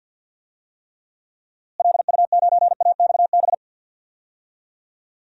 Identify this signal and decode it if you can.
Morse code, heard as a keyed tone, about 50 wpm, 700 Hz. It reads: LU9AXB
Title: DXCC-100 Call Signs - 6 Characters - Repeated 2x - 50wpm